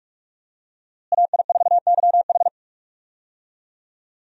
Morse code, heard as a keyed tone, about 44 wpm, 700 Hz. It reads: AI4YH
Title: USA Call Signs - 5 Characters - Repeated 2x - 45wpm